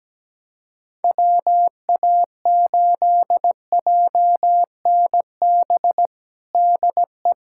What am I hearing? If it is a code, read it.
Morse code, heard as a keyed tone, about 17 wpm, 700 Hz. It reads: WA8JNB DE
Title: USA Callsign DE Callsign - 6 Characters - 17wpm